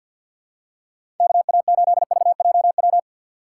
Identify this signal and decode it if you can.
Morse code, heard as a keyed tone, about 53 wpm, 700 Hz. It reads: KA8VJW
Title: USA Call Signs - 6 Characters - 6x Character Spacing - 50wpm